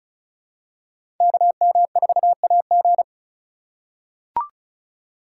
Morse code, heard as a keyed tone, about 36 wpm, 700 Hz. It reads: KM4AG
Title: USA Call Signs - 5 Characters - 2x Character Spacing - 35wpm